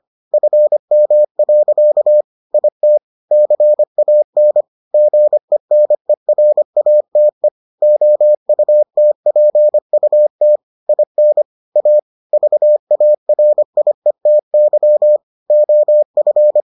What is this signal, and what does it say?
Morse code, heard as a keyed tone, about 25 wpm, 600 Hz. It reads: FM. IT CAN GENERATE OUTPUT IN A VARIETY OF